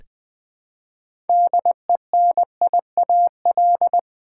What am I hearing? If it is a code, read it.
Morse code, heard as a keyed tone, about 20 wpm, 700 Hz. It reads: DENIAL
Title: Build A Word - 3 Letters At A Time - 6-Letter Words - 20wpm